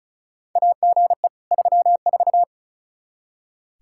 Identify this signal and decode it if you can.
Morse code, heard as a keyed tone, about 34 wpm, 700 Hz. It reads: AGE 34
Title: K1USN SST QSO Practice 35wpm